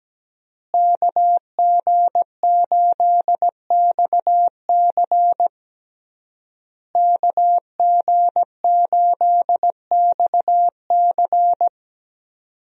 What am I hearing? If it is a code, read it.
Morse code, heard as a keyed tone, about 17 wpm, 700 Hz. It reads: KG8XC KG8XC
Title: USA Call Signs - 5 Characters - Repeated 3x - 17wpm